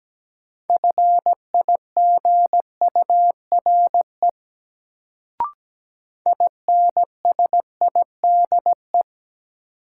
Morse code, heard as a keyed tone, about 17 wpm, 700 Hz. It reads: FIGURE INSIDE
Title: Sentences from Top 1000 Words - Review Part 2 - New Word Drill - 17wpm